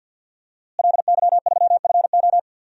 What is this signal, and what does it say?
Morse code, heard as a keyed tone, about 52 wpm, 700 Hz. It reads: LY3FO
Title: DXCC-100 Call Signs - 5 Characters - Repeated 2x - 50wpm